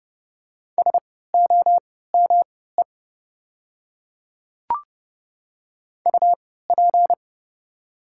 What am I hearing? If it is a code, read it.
Morse code, heard as a keyed tone, about 30 wpm, spaced as for 17 wpm, 700 Hz. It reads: SOME UP
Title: 3x Character Spacing - Top 100 Words - 30wpm